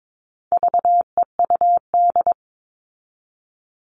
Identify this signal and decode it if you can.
Morse code, heard as a keyed tone, about 22 wpm, 700 Hz. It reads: VEUD
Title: All Cut Numbers - 4-Digit 22wpm